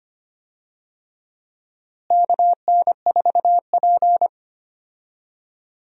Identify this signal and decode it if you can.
Morse code, heard as a keyed tone, about 25 wpm, 700 Hz. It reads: KN4P
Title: USA Call Signs - 4 Characters - Repeated 3x - 25wpm